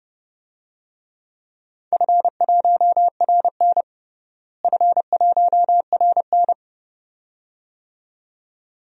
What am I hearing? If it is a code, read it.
Morse code, heard as a keyed tone, about 30 wpm, 700 Hz. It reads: F1RN F1RN
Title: DXCC-100 Call Signs - 4 Characters - Repeated 2x - 30wpm